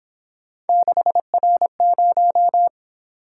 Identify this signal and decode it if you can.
Morse code, heard as a keyed tone, about 27 wpm, 700 Hz. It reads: 6R0
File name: Speed-Racing - Three Letter-Number Combinations - 22wpm